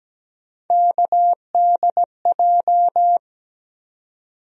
Morse code, heard as a keyed tone, about 17 wpm, 700 Hz. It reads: KDJ
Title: USA Call Sign Postfixes - 3 Letters 17wpm